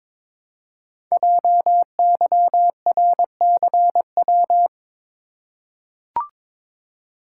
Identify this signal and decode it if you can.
Morse code, heard as a keyed tone, about 22 wpm, 700 Hz. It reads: JYRCW
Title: Code Groups - 5 Letter:Numbers - 22wpm